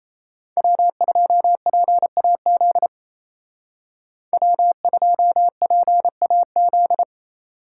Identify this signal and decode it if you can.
Morse code, heard as a keyed tone, about 36 wpm, 700 Hz. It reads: W2PAZ W2PAZ